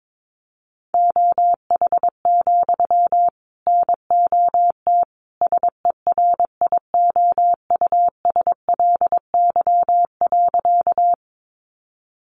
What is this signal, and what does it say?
Morse code, heard as a keyed tone, about 22 wpm, 700 Hz. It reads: OH, NOT SERIOUSLY.
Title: Sentences from Top 1200 Words 22wpm